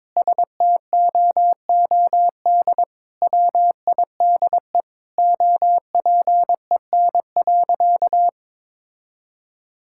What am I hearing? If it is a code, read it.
Morse code, heard as a keyed tone, about 22 wpm, 700 Hz. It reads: STOOD WIDE OPEN.
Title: Sentences from Top 1000 Words - Review Part 6 - 22wpm